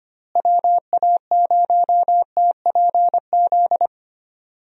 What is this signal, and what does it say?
Morse code, heard as a keyed tone, about 25 wpm, 700 Hz. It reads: WA0TPZ
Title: USA Call Signs - 6 Characters - Repeated 2x - 25wpm